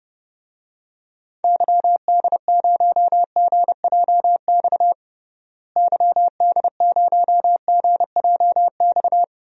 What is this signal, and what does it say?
Morse code, heard as a keyed tone, about 30 wpm, 700 Hz. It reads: YD0GJX YD0GJX
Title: DXCC-100 Call Signs - 6 Characters - Repeated 2x - 30wpm